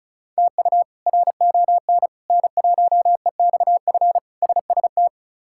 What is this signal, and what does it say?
Morse code, heard as a keyed tone, about 36 wpm, 700 Hz. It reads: TU RON N1EXF SST
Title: K1USN SST QSO Practice - 35wpm